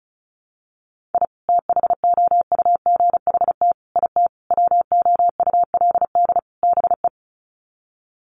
Morse code, heard as a keyed tone, about 34 wpm, 700 Hz. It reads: I THOUGHT IT WOULD BE
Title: Sets of 5 Words Encore 35wpm